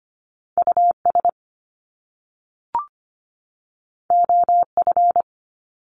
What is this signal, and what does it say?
Morse code, heard as a keyed tone, about 25 wpm, 700 Hz. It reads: US OF
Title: Top Two Letter Combinations in Words 25wpm